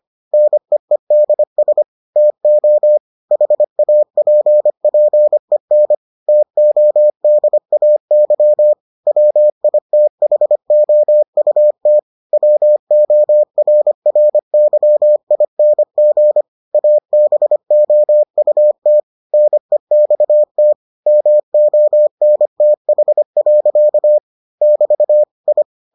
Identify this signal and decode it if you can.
Morse code, heard as a keyed tone, about 25 wpm, 600 Hz. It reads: NEEDS TO HAPPEN TODAY WITHOUT WORRYING ABOUT NEXT MONTH. <BT> I